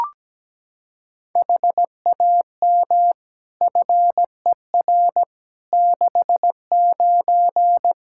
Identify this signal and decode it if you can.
Morse code, heard as a keyed tone, about 17 wpm, 700 Hz. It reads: HAM FER 69
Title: Practice QSO - Exchange 3 - More Info - 17wpm